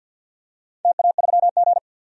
Morse code, heard as a keyed tone, about 47 wpm, 700 Hz. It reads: TA3G
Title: DXCC-100 Call Signs - 4 Characters - 50wpm